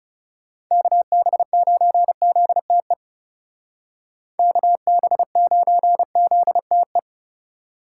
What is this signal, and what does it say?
Morse code, heard as a keyed tone, about 39 wpm, 700 Hz. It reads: KB9ZTE KB9ZTE